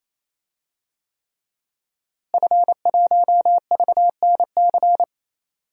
Morse code, heard as a keyed tone, about 28 wpm, 700 Hz. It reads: F1VNC
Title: DXCC-100 Call Signs - 5 Characters - Repeated 3x - 28wpm